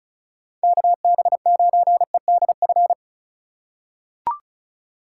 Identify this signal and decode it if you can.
Morse code, heard as a keyed tone, about 36 wpm, 700 Hz. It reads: KB9EDF
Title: USA Call Signs - 6 Characters - 2x Character Spacing - 35wpm